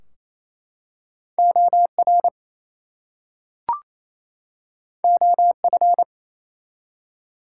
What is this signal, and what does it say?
Morse code, heard as a keyed tone, about 27 wpm, 700 Hz. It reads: OR OF
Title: Two Letter Words - 28wpm